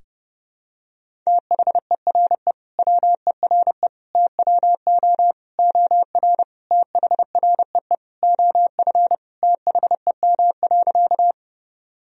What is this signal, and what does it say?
Morse code, heard as a keyed tone, about 29 wpm, 700 Hz. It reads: THERE WERE TWO OR THREE OF THEM.
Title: Sentences from Top 1000 Words - Review Part 1 - 30wpm